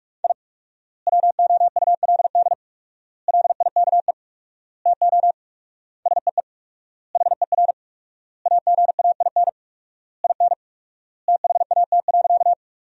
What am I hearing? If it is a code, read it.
Morse code, heard as a keyed tone, about 46 wpm, 700 Hz. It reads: I WOULD LIKE TO SEE HER AGAIN IN THAT.
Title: 4x Word-Spacing - Sentences from Top 200 Words - 45wpm